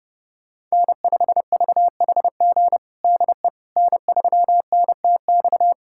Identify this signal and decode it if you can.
Morse code, heard as a keyed tone, about 32 wpm, 700 Hz. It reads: N5VHG DE N3NTX
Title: USA Callsign DE Callsign - 5 Characters - 30wpm